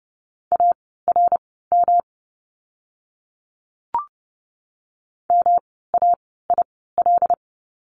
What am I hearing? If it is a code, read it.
Morse code, heard as a keyed tone, about 30 wpm, spaced as for 17 wpm, 700 Hz. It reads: ARM MAIL
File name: course / 30:10wpm / 108 Lesson 18f - Letter m - Words 30wpm